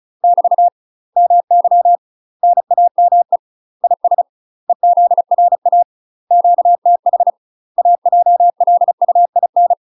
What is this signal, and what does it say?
Morse code, heard as a keyed tone, about 36 wpm, 700 Hz. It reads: <BT> MY NAME IS EZRA QTH AJLUIN
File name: UK MASTER QSOs TRAINING / 35WPM / UK MASTER QSOs 0022